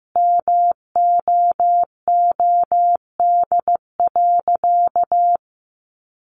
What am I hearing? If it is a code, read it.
Morse code, heard as a keyed tone, about 15 wpm, 700 Hz. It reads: MOOD.
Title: Sentences from Top 1100 Words 15wpm